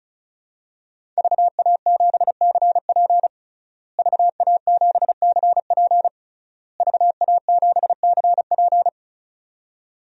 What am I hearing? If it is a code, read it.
Morse code, heard as a keyed tone, about 36 wpm, 700 Hz. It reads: VA7CP VA7CP VA7CP
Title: DXCC-100 Call Signs - 5 Characters - Repeated 3x - 35wpm